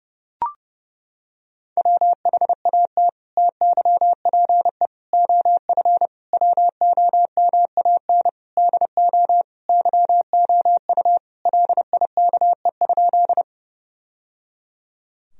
Morse code, heard as a keyed tone, about 32 wpm, 700 Hz. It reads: WHAT TYPE OF WOMAN DO YOU LIKE?
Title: Sentences from Top 1000 Words - Review Part 3 - 30wpm